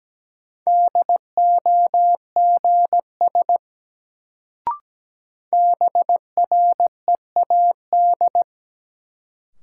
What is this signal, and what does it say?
Morse code, heard as a keyed tone, about 17 wpm, 700 Hz. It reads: DOGS BREAD
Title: Sentences from Top 1600 Words - New Word Drill Alt - 17wpm